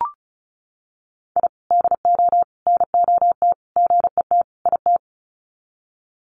Morse code, heard as a keyed tone, about 34 wpm, 700 Hz. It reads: I DO NOT GET IT
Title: Sets of 5 Words Encore 35wpm